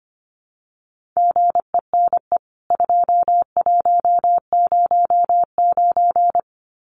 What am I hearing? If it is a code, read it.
Morse code, heard as a keyed tone, about 25 wpm, 700 Hz. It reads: GENE 2109
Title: CWT Exchange 25wpm